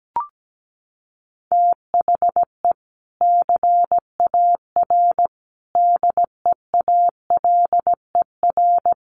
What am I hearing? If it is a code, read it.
Morse code, heard as a keyed tone, about 17 wpm, 700 Hz. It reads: THE CAR DEALER
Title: Sentences from Top 1100 Words 17wpm